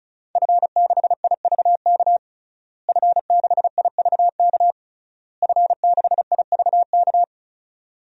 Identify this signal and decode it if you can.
Morse code, heard as a keyed tone, about 36 wpm, 700 Hz. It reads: F6IVK F6IVK F6IVK
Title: DXCC-100 Call Signs - 5 Characters - Repeated 3x - 35wpm